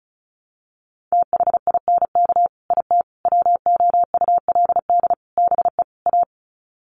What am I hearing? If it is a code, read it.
Morse code, heard as a keyed tone, about 34 wpm, 700 Hz. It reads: THINK IT WOULD BE A
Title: Sets of 5 Words Encore2 35wpm